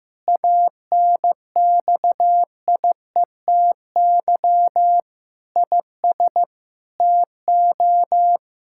Morse code, heard as a keyed tone, about 16 wpm, 700 Hz. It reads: ANXIETY IS TO
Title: Sentences from Top 1600 Words 15wpm